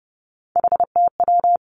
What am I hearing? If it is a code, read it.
Morse code, heard as a keyed tone, about 30 wpm, 700 Hz. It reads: HTW
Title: Three Challenging Letter-Number Combinations 30wpm